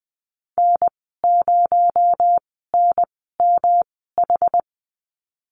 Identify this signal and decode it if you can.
Morse code, heard as a keyed tone, about 20 wpm, spaced as for 14 wpm, 700 Hz. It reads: N0NMH
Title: 245 Lesson 39g - Letter z - Call Signs 20wpm